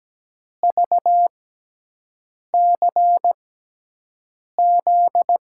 Morse code, heard as a keyed tone, about 17 wpm, spaced as for 6 wpm, 700 Hz. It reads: VCZ
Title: USA Call Signs - 6x Character Spacing - 17wpm